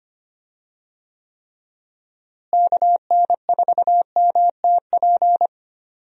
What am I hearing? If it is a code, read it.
Morse code, heard as a keyed tone, about 25 wpm, 700 Hz. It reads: KN4MTP
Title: USA Call Signs - 6 Characters - Repeated 3x - 25wpm